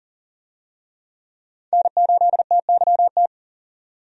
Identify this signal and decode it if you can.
Morse code, heard as a keyed tone, about 40 wpm, 700 Hz. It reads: N8TYT
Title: USA Call Signs - 5 Characters - 40wpm